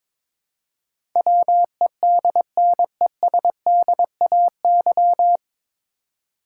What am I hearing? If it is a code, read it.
Morse code, heard as a keyed tone, about 22 wpm, 700 Hz. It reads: WEDNESDAY
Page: Morse Code Ninja practice 22wpm